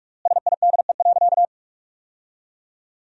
Morse code, heard as a keyed tone, about 41 wpm, 700 Hz. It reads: SIDE.
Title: 4x Word-Spacing - Sentences from Top 400 Words - 45wpm